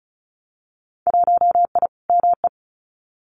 Morse code, heard as a keyed tone, about 34 wpm, 700 Hz. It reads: 1I ME
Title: Winter Field Day Exchanges 35wpm